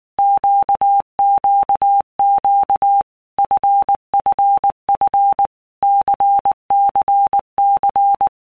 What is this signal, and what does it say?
Morse code, heard as a keyed tone, about 19 wpm, 800 Hz. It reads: QQQ FFF CCC
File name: Basic Letter Groups By Blocks / Groups@18 / LFCYQ@18